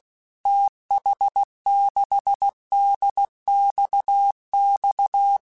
Morse code, heard as a keyed tone, about 16 wpm, 800 Hz. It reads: TH6DXX
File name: QSOs / QSOs-mp3s / WD0BGP 1@15